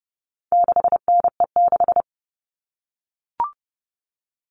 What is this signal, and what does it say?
Morse code, heard as a keyed tone, about 30 wpm, 700 Hz. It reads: 6NE6
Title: All Cut Numbers - 4-Digit 30wpm